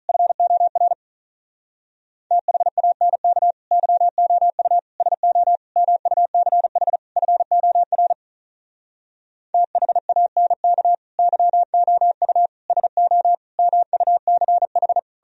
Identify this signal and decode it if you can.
Morse code, heard as a keyed tone, about 46 wpm, 700 Hz. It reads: FOR THANK YOU SO MUCH FOR THANK YOU SO MUCH